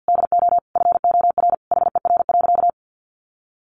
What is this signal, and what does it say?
Morse code, heard as a keyed tone, about 50 wpm, 700 Hz. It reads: DO FOR HER.
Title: Sentences from Top 600 Words 50wpm